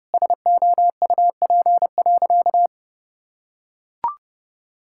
Morse code, heard as a keyed tone, about 32 wpm, 700 Hz. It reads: SOUP.
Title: Sentences from Top 1000 Words - Review Part 6 - 30wpm